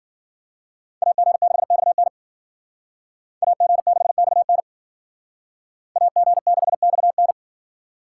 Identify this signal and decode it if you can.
Morse code, heard as a keyed tone, about 51 wpm, 700 Hz. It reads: AG6XN AG6XN AG6XN